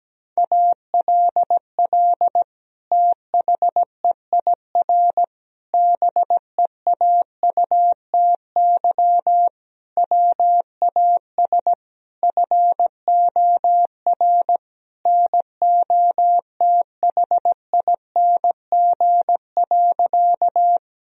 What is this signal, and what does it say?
Morse code, heard as a keyed tone, about 17 wpm, 700 Hz. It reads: ALL THEIR BEAUTY WAS FOR NOTHING.
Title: Sentences from Top 1000 Words - Review Part 2 - 17wpm